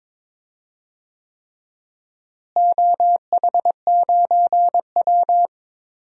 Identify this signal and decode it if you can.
Morse code, heard as a keyed tone, about 21 wpm, 700 Hz. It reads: OH9W
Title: DXCC-100 Call Signs - 4 Characters - Repeated 3x - 22wpm